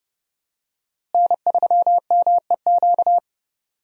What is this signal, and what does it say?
Morse code, heard as a keyed tone, about 30 wpm, 700 Hz. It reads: N3MEQ
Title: USA Call Signs - 5 Characters - Repeated 2x - 30wpm